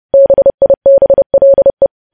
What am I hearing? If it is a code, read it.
Morse code, heard as a keyed tone, about 30 wpm, 550 Hz. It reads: BIBLE
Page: news headlines in morse code at 30 wpm